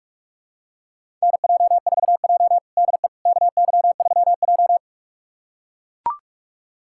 Morse code, heard as a keyed tone, about 41 wpm, 700 Hz. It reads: NJ4J DE KY3J
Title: USA Callsign DE Callsign - 4 Characters - 45wpm